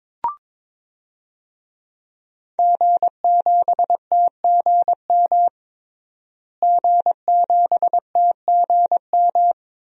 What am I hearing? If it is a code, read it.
Morse code, heard as a keyed tone, about 21 wpm, 700 Hz. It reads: G7TGM G7TGM
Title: DXCC-100 Call Signs - 5 Characters - Repeated 2x - 22wpm